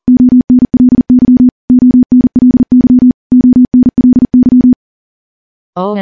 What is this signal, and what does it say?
Morse code, heard as a keyed tone, about 40 wpm, 250 Hz. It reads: ONLY ONLY ONLY
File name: morse / morse-training-40WPM